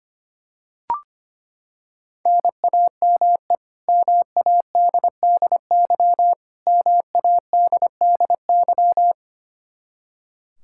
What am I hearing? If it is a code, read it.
Morse code, heard as a keyed tone, about 25 wpm, 700 Hz. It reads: NAME MADDY MADDY
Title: Practice QSO - Exchange 1 - Beginning - 25wpm